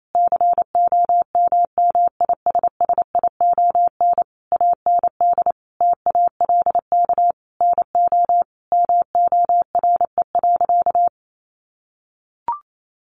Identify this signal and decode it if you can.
Morse code, heard as a keyed tone, about 28 wpm, 700 Hz. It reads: COMMISSION AND TALK NO MORE.
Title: Sentences from Top 1400 Words 28wpm